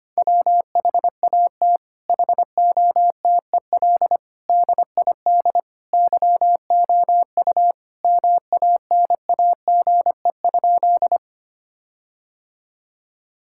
Morse code, heard as a keyed tone, about 25 wpm, 700 Hz. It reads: WHAT HOTEL DID YOU MANAGE?
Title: Sentences from Top 1600 Words 25wpm